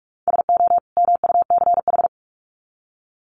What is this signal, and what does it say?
Morse code, heard as a keyed tone, about 44 wpm, 700 Hz. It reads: SO MUCH
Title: Sets of 2 Words 45wpm